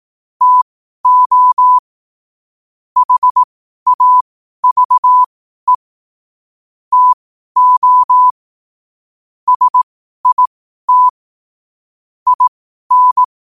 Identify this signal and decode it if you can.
Morse code, heard as a keyed tone, about 18 wpm, spaced as for 12 wpm, 1000 Hz. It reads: TO HAVE TO SIT IN
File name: qotdfiles / 12wpm